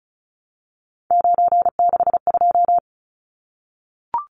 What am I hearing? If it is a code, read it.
Morse code, heard as a keyed tone, about 34 wpm, 700 Hz. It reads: 962
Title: Numbers - 3-Digit 35wpm